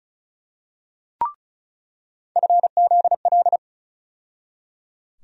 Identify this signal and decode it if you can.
Morse code, heard as a keyed tone, about 36 wpm, 700 Hz. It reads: FZL
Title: USA Call Sign Postfixes - 3 Letters 35wpm